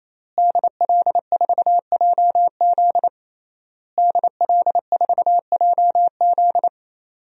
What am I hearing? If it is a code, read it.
Morse code, heard as a keyed tone, about 28 wpm, 700 Hz. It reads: DL4JZ DL4JZ
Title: DXCC-100 Call Signs - 5 Characters - Repeated 3x - 28wpm